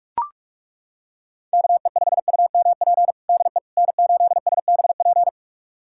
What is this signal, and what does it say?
Morse code, heard as a keyed tone, about 42 wpm, 700 Hz. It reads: KE5UMP DE N8SBP
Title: USA Callsign DE Callsign 45wpm